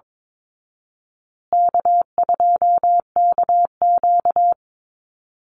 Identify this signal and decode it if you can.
Morse code, heard as a keyed tone, about 22 wpm, 700 Hz. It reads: K2KQ
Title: Easy Call Signs Repeated 2x - 22wpm